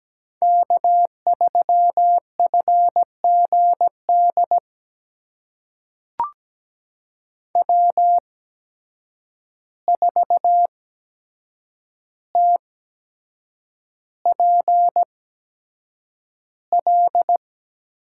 Morse code, heard as a keyed tone, about 17 wpm, spaced as for 7 wpm, 700 Hz. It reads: K3FGD W4TPL
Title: USA Call Signs - 5 Characters - 8x Character Spacing - 17wpm